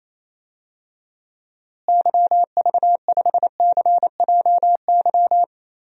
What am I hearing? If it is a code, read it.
Morse code, heard as a keyed tone, about 28 wpm, 700 Hz. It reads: YV5CJY
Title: DXCC-100 Call Signs - 6 Characters - Repeated 2x - 28wpm